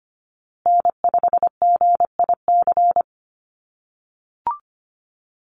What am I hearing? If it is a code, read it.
Morse code, heard as a keyed tone, about 25 wpm, 700 Hz. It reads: N5GIC
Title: USA Call Signs Repeated 2x 25wpm